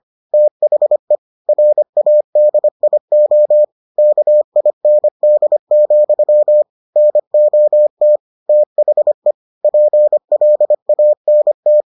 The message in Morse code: THE RADIO KIND, NOT THE PLANT

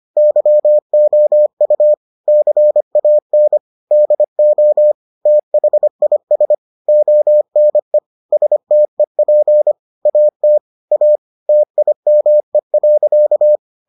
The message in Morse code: YOU CAN DO THIS ONE STEP AT A TIME.

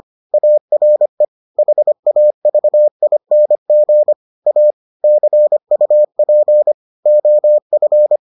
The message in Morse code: ARE HAVING A CUP OF

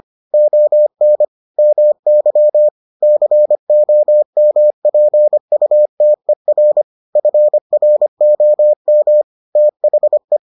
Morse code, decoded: ON MY COMPUTER FROM THE

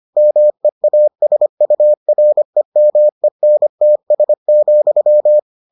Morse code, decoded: MEASUREMENTS,

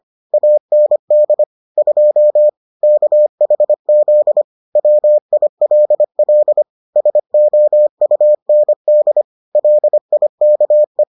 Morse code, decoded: AND 2 KHZ WILL SOUND LIKE

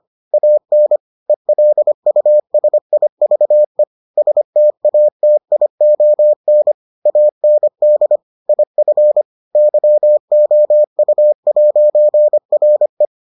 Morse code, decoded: AN ELUSIVE STATION AND IF YOU'RE

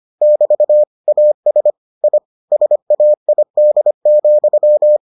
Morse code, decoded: <BT> AS I SAID,

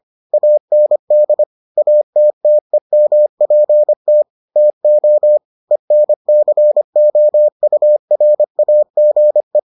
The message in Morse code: AND ATTEMPT TO ENCOURAGE

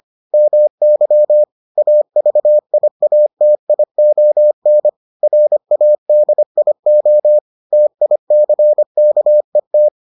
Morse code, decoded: MY AVIATION RADIO TICKET

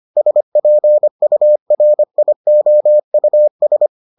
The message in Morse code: SPURIOUS

